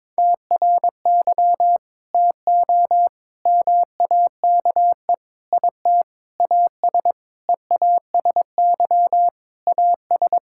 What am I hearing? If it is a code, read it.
Morse code, heard as a keyed tone, about 22 wpm, 700 Hz. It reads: TRY TO MAKE IT AS EASY AS
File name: practice / 22wpm / Sentences with 8-Word Phrases - 22wpm